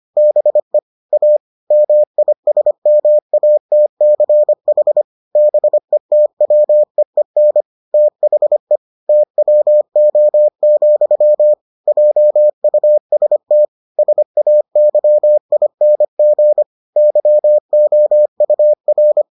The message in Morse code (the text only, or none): BE A MISMATCH BETWEEN THE TWO, JUST SAYING YOUR